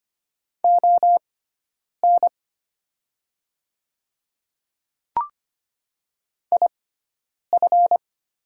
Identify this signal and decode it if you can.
Morse code, heard as a keyed tone, about 25 wpm, spaced as for 9 wpm, 700 Hz. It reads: ON IF
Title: 6x Character Spacing - Top 100 Words - 25wpm